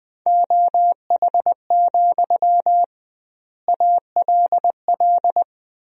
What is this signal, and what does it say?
Morse code, heard as a keyed tone, about 20 wpm, 700 Hz. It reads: OH, ALL
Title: 2x Word-Spacing - Sentences from Top 500 Words - 20wpm